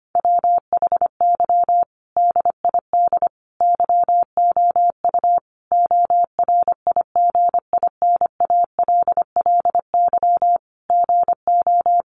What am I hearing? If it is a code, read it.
Morse code, heard as a keyed tone, about 25 wpm, 700 Hz. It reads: WHY DID YOU ORIGINALLY GO
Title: Sentences from Top 1500 Words 25wpm